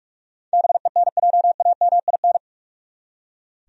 Morse code, heard as a keyed tone, about 41 wpm, 700 Hz. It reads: BENJAMIN
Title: USA Common Names - 45wpm